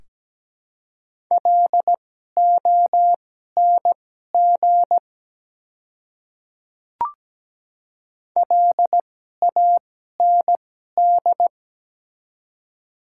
Morse code, heard as a keyed tone, about 17 wpm, spaced as for 12 wpm, 700 Hz. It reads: LONG LAND
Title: 2x Character Spacing - Top 200 Words - 17wpm